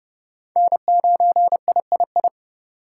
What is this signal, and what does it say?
Morse code, heard as a keyed tone, about 30 wpm, 700 Hz. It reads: N9III
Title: USA Call Signs - 5 Characters - 4x Character Spacing - 30wpm